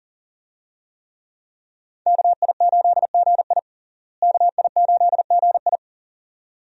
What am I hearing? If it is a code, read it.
Morse code, heard as a keyed tone, about 40 wpm, 700 Hz. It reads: KI8GI KI8GI
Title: USA Call Signs - 5 Characters - Repeated 2x - 40wpm